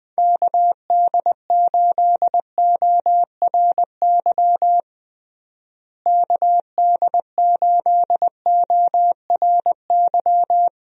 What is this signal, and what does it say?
Morse code, heard as a keyed tone, about 20 wpm, 700 Hz. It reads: KD8ORY KD8ORY
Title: USA Call Signs - 6 Characters - Repeated 2x - 20wpm